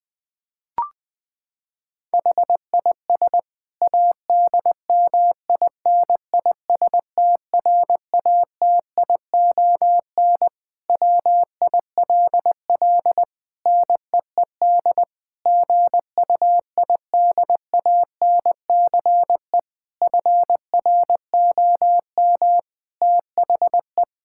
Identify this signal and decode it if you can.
Morse code, heard as a keyed tone, about 20 wpm, 700 Hz. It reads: HIS ADMINISTRATION WILL NEED GUIDANCE FROM THE
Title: Sentences from Top 1600 Words 20wpm